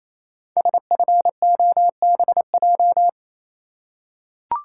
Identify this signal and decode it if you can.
Morse code, heard as a keyed tone, about 28 wpm, 700 Hz. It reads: SFOBJ
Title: Code Groups - 5 Letters - 28wpm